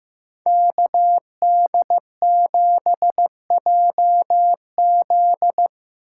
Morse code, heard as a keyed tone, about 19 wpm, 700 Hz. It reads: KD7JZ
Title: Speed Racing - USA Call Signs 15wpm